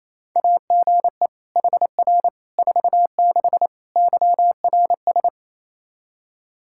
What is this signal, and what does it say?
Morse code, heard as a keyed tone, about 28 wpm, 700 Hz. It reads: AGE HR 46 YRS
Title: Practice QSO - Exchange 3 - More Info - 28wpm